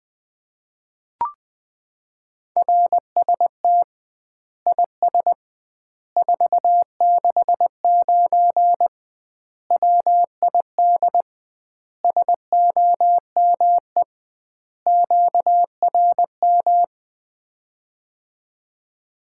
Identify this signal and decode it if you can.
Morse code, heard as a keyed tone, about 21 wpm, 700 Hz. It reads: RST IS 469 WID SOME QRM
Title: Common QSO Phrases - 2x Word-Spacing - 20wpm